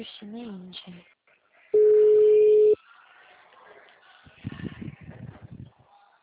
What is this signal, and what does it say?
Ringback tone (Europe, 425 Hz), with speech.
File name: out- -2031